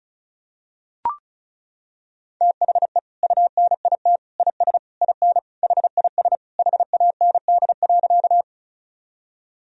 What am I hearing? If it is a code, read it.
Morse code, heard as a keyed tone, about 36 wpm, 700 Hz. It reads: THE UNIT IS IN HIS HAND.
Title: Sentences from Top 1000 Words - Review Part 2 - 35wpm